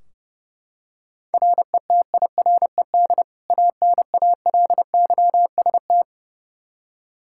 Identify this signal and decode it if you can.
Morse code, heard as a keyed tone, about 29 wpm, 700 Hz. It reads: RETIRED ANALYST
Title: K1USN SST QSO Practice 30wpm